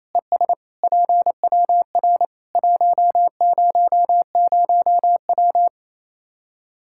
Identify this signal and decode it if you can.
Morse code, heard as a keyed tone, about 28 wpm, 700 Hz. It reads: ES PWR 100W
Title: Practice QSO - Exchange 2 - Info - 28wpm